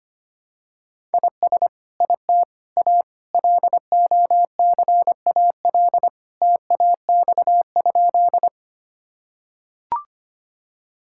Morse code, heard as a keyed tone, about 25 wpm, 700 Hz. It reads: IS IT A LOCAL TAX?
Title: Sentences from Top 1000 Words - Review Part 5 - 25wpm